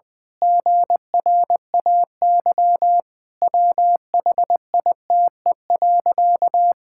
Morse code, heard as a keyed tone, about 19 wpm, 700 Hz. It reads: GRAY WHITE.
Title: Pangram Challenge v2 - 20wpm